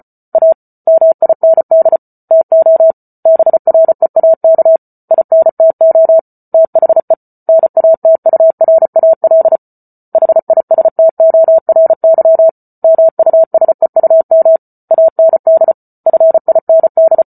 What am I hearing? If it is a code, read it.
Morse code, heard as a keyed tone, about 34 wpm, 650 Hz. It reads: A MIND TO BREAK INTO THE NATURAL HISTORY MUSEUM AND FIND